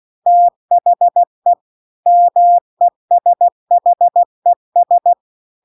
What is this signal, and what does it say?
Morse code, heard as a keyed tone, about 16 wpm, 700 Hz. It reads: THE MESHES
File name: CW-The-Big-Twoheaded-River-2-16WPM